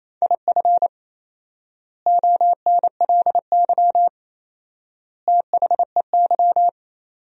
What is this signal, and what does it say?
Morse code, heard as a keyed tone, about 28 wpm, 700 Hz. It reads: IF ONLY THEY
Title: 4x Word-Spacing - Sentences from Top 500 Words - 28wpm